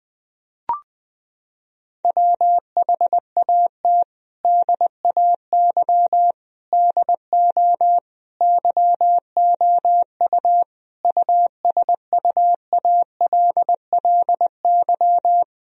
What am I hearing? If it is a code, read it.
Morse code, heard as a keyed tone, about 21 wpm, 700 Hz. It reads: WHAT DAY DO YOU USUALLY
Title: Sentences from Top 1000 Words - Review Part 3 - 20wpm